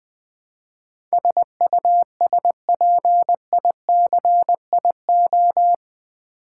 Morse code, heard as a keyed tone, about 20 wpm, 700 Hz. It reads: SUSPICIO
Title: Build A Word - 4 Letters At A Time - 10-Letter Words - 20wpm